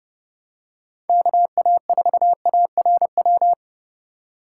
Morse code, heard as a keyed tone, about 30 wpm, 700 Hz. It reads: KA4ARW